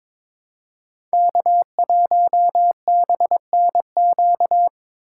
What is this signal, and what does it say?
Morse code, heard as a keyed tone, about 22 wpm, 700 Hz. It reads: K1BNQ
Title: USA Call Signs - 5 Characters - 6x Character Spacing - 22wpm